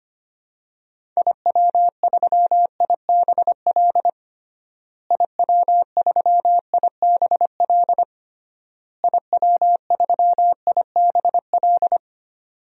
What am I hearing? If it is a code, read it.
Morse code, heard as a keyed tone, about 25 wpm, 700 Hz. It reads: IW3IBL IW3IBL IW3IBL
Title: DXCC-100 Call Signs - 6 Characters - Repeated 3x - 25wpm